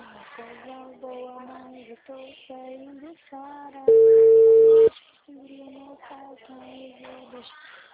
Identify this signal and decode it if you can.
Ringback tone (China, 450 Hz), with speech.